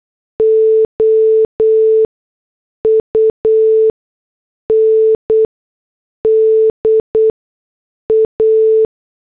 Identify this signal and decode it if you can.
Morse code, heard as a keyed tone, about 8 wpm, spaced as for 6 wpm, 440 Hz. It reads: OUNDA